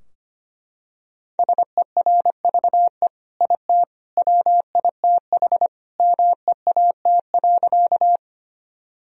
Morse code, heard as a keyed tone, about 25 wpm, 700 Hz. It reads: SERVE IT WITH MEAT.
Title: Sentences from Top 1000 Words - Review Part 3 - 25wpm